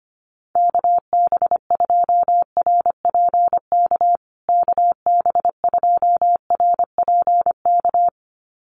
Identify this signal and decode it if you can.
Morse code, heard as a keyed tone, about 25 wpm, 700 Hz. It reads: KB2RPK KB2RPK
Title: USA Call Signs Repeated 2x 25wpm